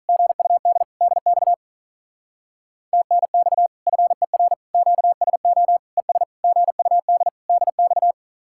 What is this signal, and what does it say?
Morse code, heard as a keyed tone, about 44 wpm, 700 Hz. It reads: GUD DX TNX FER QSO ES GUD DX